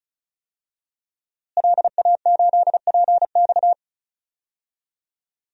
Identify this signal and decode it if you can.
Morse code, heard as a keyed tone, about 36 wpm, 700 Hz. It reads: LA8PX
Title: DXCC-100 Call Signs - 5 Characters - Repeated 3x - 35wpm